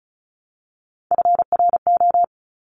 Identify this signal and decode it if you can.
Morse code, heard as a keyed tone, about 36 wpm, 700 Hz. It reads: FRO
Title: Top Three Letter Combinations in Words 35wpm